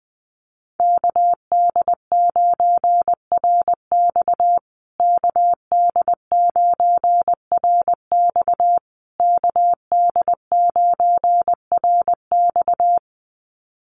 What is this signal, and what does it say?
Morse code, heard as a keyed tone, about 19 wpm, 700 Hz. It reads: KD9RX KD9RX KD9RX
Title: USA Call Signs Repeated 3x 20wpm